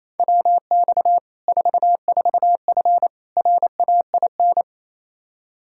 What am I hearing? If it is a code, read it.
Morse code, heard as a keyed tone, about 28 wpm, 700 Hz. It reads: WX 44F RAIN
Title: Practice QSO - Exchange 2 - Info - 28wpm